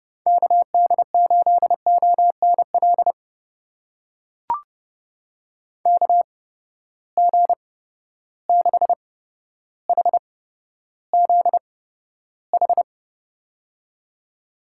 Morse code, heard as a keyed tone, about 30 wpm, spaced as for 13 wpm, 700 Hz. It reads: KD8ONL KG6HZH
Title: USA Call Signs - 6 Characters - 8x Character Spacing - 30wpm